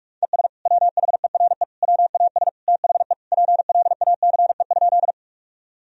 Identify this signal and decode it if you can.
Morse code, heard as a keyed tone, about 46 wpm, 700 Hz. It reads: ES WHERE WAS THE PLACE?
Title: Sentences from Top 200 Words and Abbreviations 45wpm